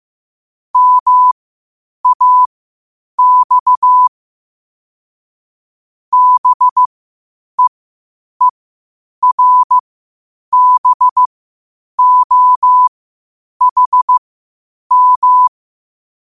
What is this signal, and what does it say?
Morse code, heard as a keyed tone, about 15 wpm, spaced as for 8 wpm, 1000 Hz. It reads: MAX BEERBOHM
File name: qotdfiles / 8wpm